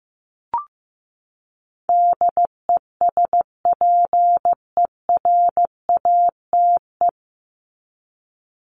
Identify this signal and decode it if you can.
Morse code, heard as a keyed tone, about 15 wpm, 700 Hz. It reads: DESPERATE
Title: Top 1751-2000 Words 15wpm